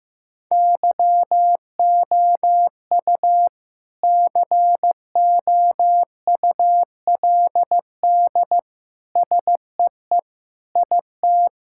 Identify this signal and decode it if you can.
Morse code, heard as a keyed tone, about 15 wpm, 700 Hz. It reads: YOU COULD SEE IT
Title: Sentences from Top 1000 Words - Review Part 5 - 15wpm